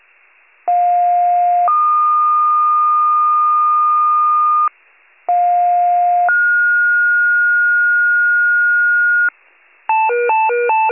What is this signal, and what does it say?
Two-tone page, heard as a one-second tone, then a three-second tone, about 710 then 1220 Hz, about 710 then 1430 Hz, with noise. It is heard from 0.7 to 9.3 s.
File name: wcalert